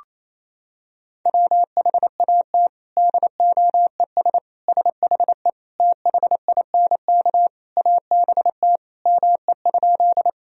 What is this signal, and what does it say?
Morse code, heard as a keyed tone, about 28 wpm, 700 Hz. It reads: WHAT DOES SHE THINK ABT ME?
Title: Sentences from Top 200 Words and Abbreviations 28wpm